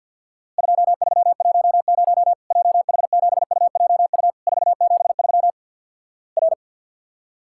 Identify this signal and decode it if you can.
Morse code, heard as a keyed tone, about 50 wpm, 700 Hz. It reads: 2310 JH7UJU 473 R
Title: Worked All Europe DX Contest - QTC Exchange with Repeats - 50wpm